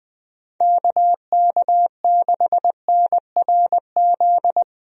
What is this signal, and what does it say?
Morse code, heard as a keyed tone, about 20 wpm, 700 Hz. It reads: KK6NRZ
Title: USA Call Signs - 6 Characters - Repeated 2x - 20wpm